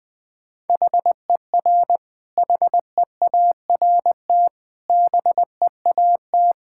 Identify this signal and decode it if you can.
Morse code, heard as a keyed tone, about 20 wpm, 700 Hz. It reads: HER HEART BEAT
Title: Sentences from Top 1600 Words 20wpm